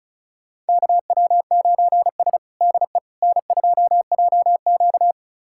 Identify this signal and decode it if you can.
Morse code, heard as a keyed tone, about 36 wpm, 700 Hz. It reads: KW9S DE N2JQ
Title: USA Callsign DE Callsign - 4 Characters - 35wpm